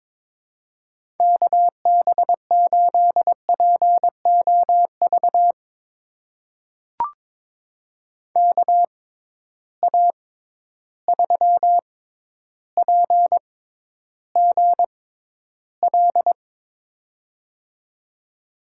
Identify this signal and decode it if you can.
Morse code, heard as a keyed tone, about 21 wpm, spaced as for 11 wpm, 700 Hz. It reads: KB8POV KA3PGL
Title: USA Call Signs - 6 Characters - 6x Character Spacing - 22wpm